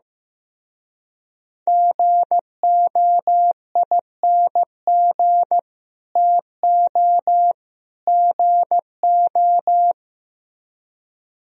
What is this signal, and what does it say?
Morse code, heard as a keyed tone, about 15 wpm, 700 Hz. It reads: GOING TO GO
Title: Speed-Racing - Sets of 3 Words 15wpm